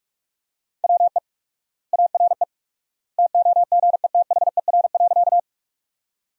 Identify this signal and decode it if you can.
Morse code, heard as a keyed tone, about 41 wpm, 700 Hz. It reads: WE ARE TOGETHER.
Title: 4x Word-Spacing - Sentences from Top 300 Words - 45wpm